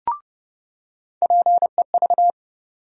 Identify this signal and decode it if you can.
Morse code, heard as a keyed tone, about 30 wpm, 700 Hz. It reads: PEV
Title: USA Call Sign Postfixes - 3 Letters 30wpm